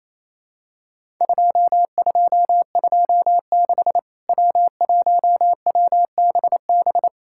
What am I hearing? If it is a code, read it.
Morse code, heard as a keyed tone, about 28 wpm, 700 Hz. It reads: 2226 W1WBB
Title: Worked All Europe DX Contest - QTC Exchange - 28wpm